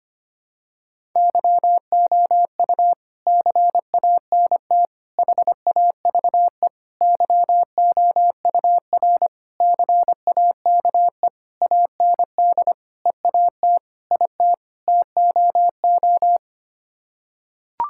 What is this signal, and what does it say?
Morse code, heard as a keyed tone, about 25 wpm, 700 Hz. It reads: YOU CANT HAVE YOUR CAKE AND EAT IT TOO
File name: Sets of 9 Words - 25wpm